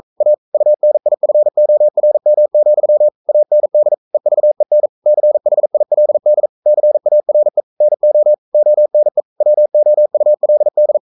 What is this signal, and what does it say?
Morse code, heard as a keyed tone, about 44 wpm, 600 Hz. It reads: A UNIFORM, AND EVEN CHILD CARE NO ONE WOULD
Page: Romanen Never af Ken Follet på engelsk.